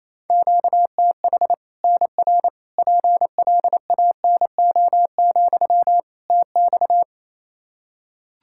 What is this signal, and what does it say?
Morse code, heard as a keyed tone, about 28 wpm, 700 Hz. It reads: QTH NR PLANO, TX
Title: Practice QSO - Exchange 1 - Beginning - 28wpm